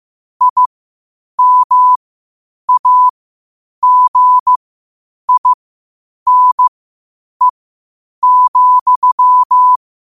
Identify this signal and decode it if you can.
Morse code, heard as a keyed tone, about 15 wpm, spaced as for 8 wpm, 1000 Hz. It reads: IMAGINE,